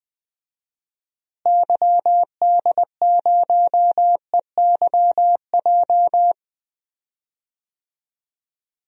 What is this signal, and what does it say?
Morse code, heard as a keyed tone, about 20 wpm, 700 Hz. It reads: YD0EYJ
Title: DXCC-100 Call Signs - 6 Characters - Repeated 2x - 20wpm